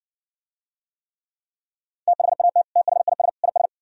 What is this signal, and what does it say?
Morse code, heard as a keyed tone, about 59 wpm, 700 Hz. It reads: THAT THIS IS